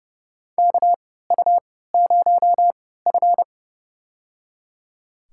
Morse code, heard as a keyed tone, about 30 wpm, spaced as for 17 wpm, 700 Hz. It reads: KU0F
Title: USA Call Signs - 3x Character Spacing - 30wpm